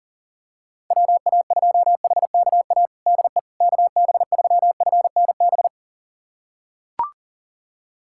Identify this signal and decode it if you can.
Morse code, heard as a keyed tone, about 40 wpm, 700 Hz. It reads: WA2HKA DE KB3FNB
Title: USA Callsign DE Callsign - 6 Characters - 40wpm